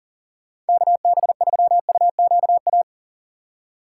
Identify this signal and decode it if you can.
Morse code, heard as a keyed tone, about 49 wpm, 700 Hz. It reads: KB3UQA
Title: Speed Racing - USA 2x3 Call Signs 40wpm